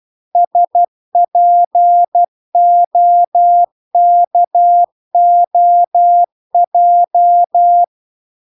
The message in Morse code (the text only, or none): SPOKOJ